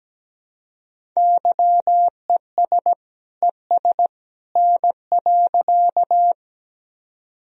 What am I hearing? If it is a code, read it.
Morse code, heard as a keyed tone, about 17 wpm, 700 Hz. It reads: YES ES N.
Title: Sentences from Top 500 Words and Abbreviations 17wpm